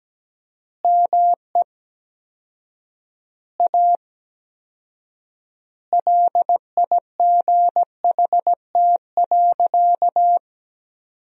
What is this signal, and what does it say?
Morse code, heard as a keyed tone, about 17 wpm, 700 Hz. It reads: ME A LIGHT.
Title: 4x Word-Spacing - Sentences from Top 200 Words - 17wpm